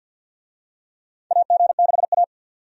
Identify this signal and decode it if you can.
Morse code, heard as a keyed tone, about 52 wpm, 700 Hz. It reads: AG6A
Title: USA Call Signs - 4 Characters - 50wpm